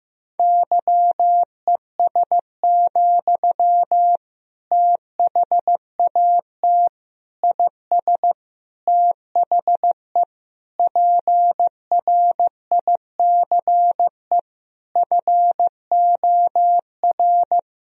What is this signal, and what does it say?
Morse code, heard as a keyed tone, about 15 wpm, 700 Hz. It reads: YES, THAT IS THE PRICE FOR
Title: Sentences from Top 1000 Words - Review Part 4 - 15wpm